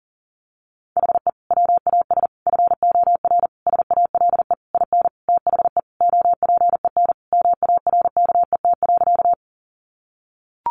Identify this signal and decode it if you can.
Morse code, heard as a keyed tone, about 40 wpm, 700 Hz. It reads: HE WAS FOR SALE IN THE OPEN MARKET.
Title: Sentences from Top 1400 Words 40wpm